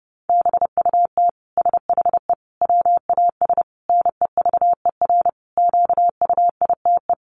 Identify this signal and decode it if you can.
Morse code, heard as a keyed tone, about 30 wpm, 700 Hz. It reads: BUT SHE WAS NEVER QUITE
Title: Sentences from Top 700 Words 30wpm